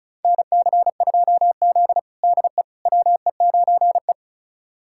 Morse code, heard as a keyed tone, about 36 wpm, 700 Hz. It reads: NC2Z DE WE9E
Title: USA Callsign DE Callsign - 4 Characters - 35wpm